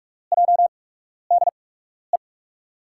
Morse code, heard as a keyed tone, about 44 wpm, spaced as for 12 wpm, 700 Hz. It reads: JDE
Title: USA Call Signs - 5 Characters - 8x Character Spacing - 45wpm